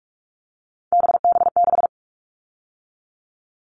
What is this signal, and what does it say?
Morse code, heard as a keyed tone, about 44 wpm, 700 Hz. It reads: BB6
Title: All Cut Numbers - 3-Digit 45wpm